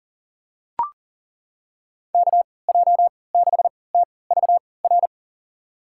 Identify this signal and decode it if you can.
Morse code, heard as a keyed tone, about 37 wpm, spaced as for 22 wpm, 700 Hz. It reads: KJ6TVR
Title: USA Call Signs - 6 Characters - 3x Character Spacing - 40wpm